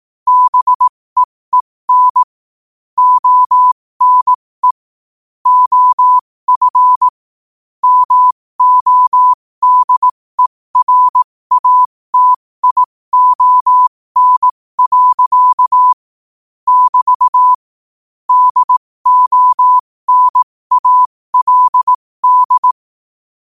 Morse code, heard as a keyed tone, about 18 wpm, spaced as for 15 wpm, 1000 Hz. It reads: BEEN ONE OF MODERATION. <BT> DONALD
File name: qotdfiles / 15wpm